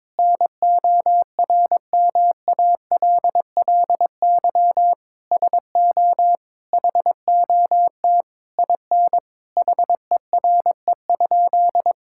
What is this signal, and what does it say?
Morse code, heard as a keyed tone, about 22 wpm, 700 Hz. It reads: NORMALLY SO HOT IN HERE?
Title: Sentences from Top 1000 Words - Review Part 6 - 22wpm